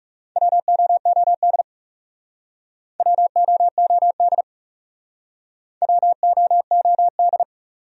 Morse code, heard as a keyed tone, about 39 wpm, 700 Hz. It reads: WOOD WOOD WOOD